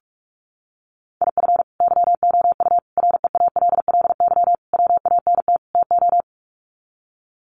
Morse code, heard as a keyed tone, about 44 wpm, 700 Hz. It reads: IF YOU REALLY WANT TO
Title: Sets of 5 Words Encore2 45wpm